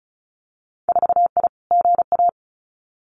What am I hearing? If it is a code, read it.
Morse code, heard as a keyed tone, about 36 wpm, 700 Hz. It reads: 4I GA
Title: Winter Field Day Exchanges Encore 35wpm